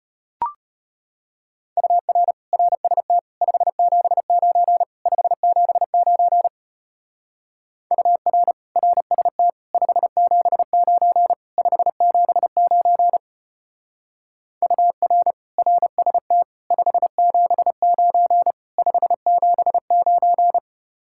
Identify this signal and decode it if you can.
Morse code, heard as a keyed tone, about 31 wpm, 700 Hz. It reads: UR RST 579 579 UR RST 579 579 UR RST 579 579
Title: Speed-Racing Common QSO Phrases Encore - 25wpm